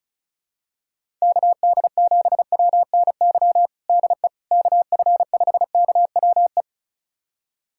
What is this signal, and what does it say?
Morse code, heard as a keyed tone, about 36 wpm, 700 Hz. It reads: KD7WNY DE KF5KWE
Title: USA Callsign DE Callsign - 6 Characters - 35wpm